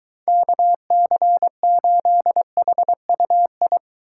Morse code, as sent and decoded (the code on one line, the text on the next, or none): -.- -.-. ---.. .... ..- ..
KC8HUI